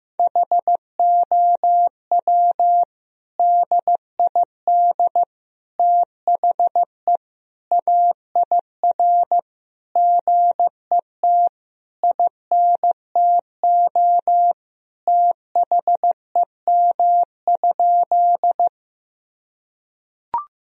Morse code, heard as a keyed tone, about 16 wpm, 700 Hz. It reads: HOW DID THE AIR GET INTO THEM?
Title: Sentences from Top 1000 Words - Review Part 1 - 15wpm